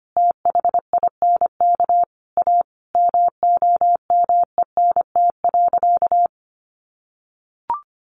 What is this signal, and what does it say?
Morse code, heard as a keyed tone, about 25 wpm, 700 Hz. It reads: THINK A MOMENT.
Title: Sentences from Top 1000 Words 25wpm